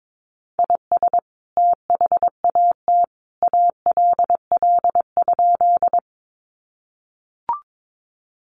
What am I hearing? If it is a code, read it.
Morse code, heard as a keyed tone, about 22 wpm, 700 Hz. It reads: IS THAT ALL?
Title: Sentences from Top 100 Words 22wpm